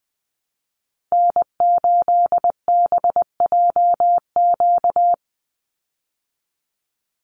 Morse code, heard as a keyed tone, about 20 wpm, 700 Hz. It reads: N8BJQ
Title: Call Signs Repeated 3x - 20wpm